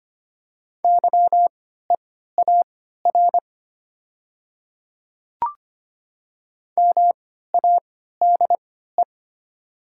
Morse code, heard as a keyed tone, about 25 wpm, spaced as for 14 wpm, 700 Hz. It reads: YEAR MADE
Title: 3x Character Spacing - Top 200 Words - 25wpm